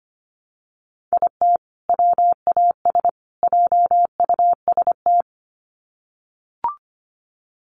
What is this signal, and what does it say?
Morse code, heard as a keyed tone, about 25 wpm, 700 Hz. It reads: IT WAS JUST
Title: Sets of 3 Words 25wpm